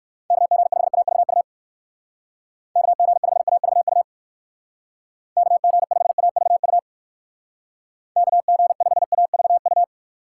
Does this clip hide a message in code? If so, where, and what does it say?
Yes: Morse code from 0.3 to 9.8 s, heard as a keyed tone, about 59 wpm, 700 Hz. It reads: KG5AVU KG5AVU KG5AVU KG5AVU